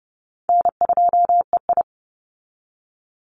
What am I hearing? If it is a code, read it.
Morse code, heard as a keyed tone, about 30 wpm, 700 Hz. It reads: N2EI
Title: Easy Call Signs Repeated 2x - 30wpm